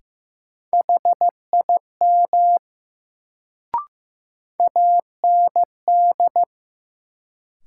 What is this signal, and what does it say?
Morse code, heard as a keyed tone, about 15 wpm, 700 Hz. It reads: HIM AND
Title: Sentences from Top 100 Words - New Word Drill - 15wpm